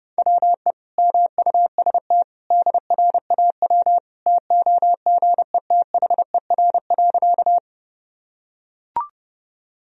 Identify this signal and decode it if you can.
Morse code, heard as a keyed tone, about 30 wpm, 700 Hz. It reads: WE MUST DRAW TOGETHER.
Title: Sentences from Top 1000 Words - Review Part 1 - 30wpm